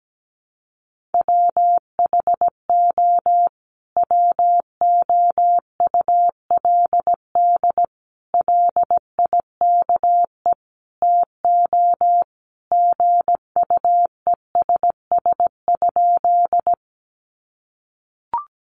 Morse code, heard as a keyed tone, about 17 wpm, 700 Hz. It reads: WHO WOULD LIKE TO GUESS?
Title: Sentences from Top 1500 Words 17wpm